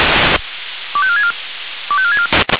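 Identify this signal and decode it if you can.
Five-tone selective call (ZVEI), heard as quick runs of five tones, tones of about 70 ms. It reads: 25464 25464
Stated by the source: ZVEI five-tone selective call received on an SDR.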